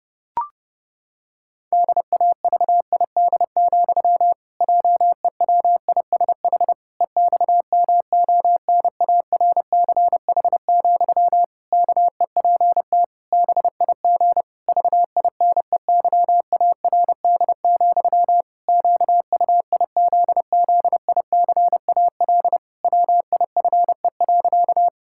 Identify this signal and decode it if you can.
Morse code, heard as a keyed tone, about 30 wpm, 700 Hz. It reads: DAVID, JEWISH EXMONARCH, KEPT BIG VINEYARD, QUIZZICAL WIFE.
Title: Pangram Challenge v2 - 30wpm